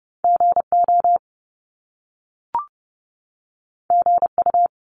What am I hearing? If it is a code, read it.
Morse code, heard as a keyed tone, about 30 wpm, 700 Hz. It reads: GO GU
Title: Top Two Letter Combinations in Words 30wpm